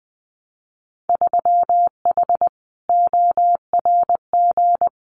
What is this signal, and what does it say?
Morse code, heard as a keyed tone, about 20 wpm, 700 Hz. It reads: 3H ORG
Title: Winter Field Day Exchanges Encore 20wpm